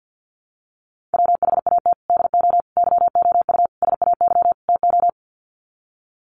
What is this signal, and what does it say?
Morse code, heard as a keyed tone, about 50 wpm, 700 Hz. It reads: WHAT DO YOU SAY TO
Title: Sets of 5 Words 50wpm